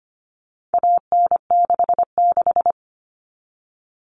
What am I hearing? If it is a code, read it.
Morse code, heard as a keyed tone, about 25 wpm, 700 Hz. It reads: AN66
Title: All Cut Numbers - 4-Digit 25wpm